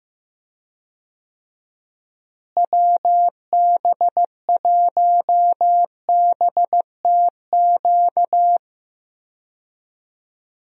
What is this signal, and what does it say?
Morse code, heard as a keyed tone, about 15 wpm, 700 Hz. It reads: WB1BTQ
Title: USA Call Signs - 6 Characters - Repeated 3x - 15wpm